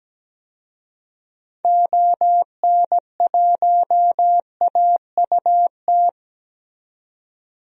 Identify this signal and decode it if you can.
Morse code, heard as a keyed tone, about 17 wpm, 700 Hz. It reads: ON1AUT
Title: DXCC-100 Call Signs - 6 Characters - 17wpm